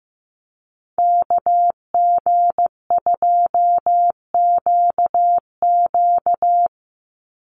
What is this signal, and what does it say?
Morse code, heard as a keyed tone, about 15 wpm, 700 Hz. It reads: KG2QQ
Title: USA Call Signs 15wpm